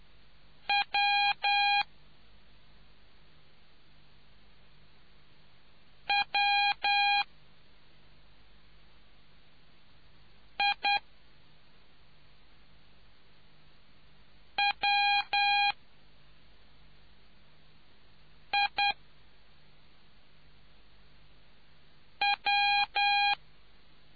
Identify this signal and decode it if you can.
Morse code, heard as a keyed tone, about 10 wpm, 810 Hz. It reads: W W I W I W